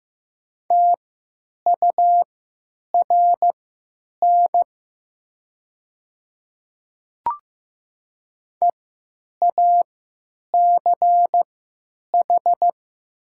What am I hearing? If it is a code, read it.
Morse code, heard as a keyed tone, about 15 wpm, spaced as for 8 wpm, 700 Hz. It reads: TURN EACH
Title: 3x Character Spacing - Top 200 Words - 15wpm